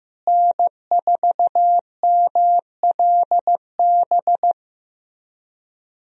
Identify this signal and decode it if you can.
Morse code, heard as a keyed tone, about 16 wpm, 700 Hz. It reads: N4MLB
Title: USA Call Signs - 5 Characters - Repeated 3x - 15wpm